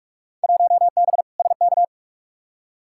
Morse code, heard as a keyed tone, about 46 wpm, 700 Hz. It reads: 1B SK
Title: Field Day Exchanges Encore - 45wpm